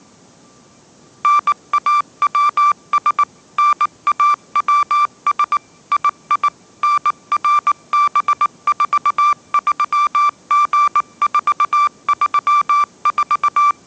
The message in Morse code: NAWS NAWS II NRB43G434

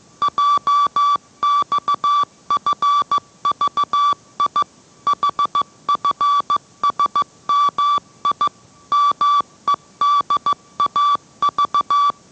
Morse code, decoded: JXFVI HFSMI MEDAV

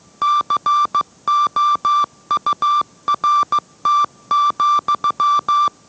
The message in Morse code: COURT,

